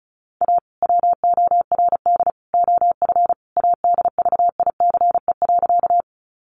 Morse code, heard as a keyed tone, about 34 wpm, 700 Hz. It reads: A WORD OF ADVICE.
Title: Sentences from Top 1100 Words 35wpm